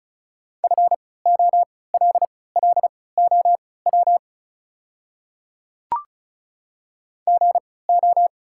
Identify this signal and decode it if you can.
Morse code, heard as a keyed tone, about 36 wpm, spaced as for 20 wpm, 700 Hz. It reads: FOLLOW GO
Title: 3x Character Spacing - Top 200 Words - 35wpm